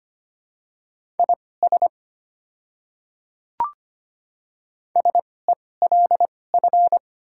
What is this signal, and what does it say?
Morse code, heard as a keyed tone, about 25 wpm, spaced as for 18 wpm, 690 Hz. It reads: IS SELF
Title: 2x Character Spacing - Top 200 Words - 25wpm